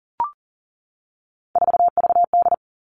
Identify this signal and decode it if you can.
Morse code, heard as a keyed tone, about 38 wpm, 700 Hz. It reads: 4VD
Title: Three Challenging Letter-Number Combinations 40wpm